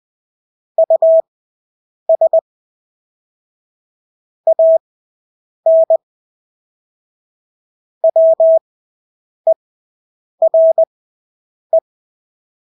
Morse code, heard as a keyed tone, about 20 wpm, spaced as for 8 wpm, 650 Hz. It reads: US AN WERE
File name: Engelske ord 20/8 wpm